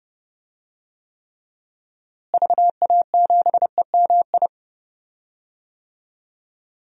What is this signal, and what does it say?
Morse code, heard as a keyed tone, about 32 wpm, 700 Hz. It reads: VA7EMI
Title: DXCC-100 Call Signs - 6 Characters - Repeated 2x - 30wpm